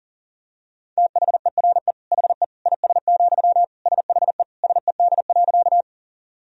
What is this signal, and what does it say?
Morse code, heard as a keyed tone, about 40 wpm, 700 Hz. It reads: THERE HE IS, SHE SED.
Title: Sentences from Top 100 Words and Abbreviations 40wpm